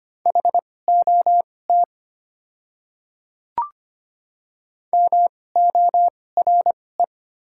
Morse code, heard as a keyed tone, about 25 wpm, spaced as for 18 wpm, 690 Hz. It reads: HOT MORE
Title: 2x Character Spacing - Top 100 Words - 25wpm